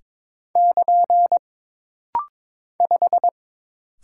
Morse code, heard as a keyed tone, about 22 wpm, 700 Hz. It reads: <KN> 5
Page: This rapid-fire timing is geared to push your ICR (Instant Character Recognition) skills.